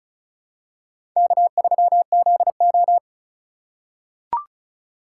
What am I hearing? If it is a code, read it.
Morse code, heard as a keyed tone, about 36 wpm, 700 Hz. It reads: K3ZO
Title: USA Call Signs - 4 Characters - 4x Character Spacing - 35wpm